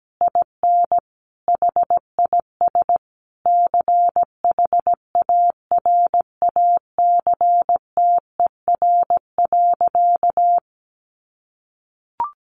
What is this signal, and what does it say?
Morse code, heard as a keyed tone, about 17 wpm, 700 Hz. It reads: IN HIS CHARACTER.
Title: Sentences from Top 1200 Words 17wpm